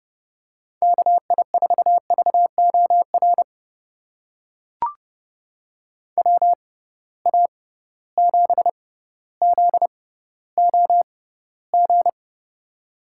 Morse code, heard as a keyed tone, about 32 wpm, spaced as for 16 wpm, 700 Hz. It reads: KI4VOR WA7ZOG
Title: USA Call Signs - 6 Characters - 6x Character Spacing - 30wpm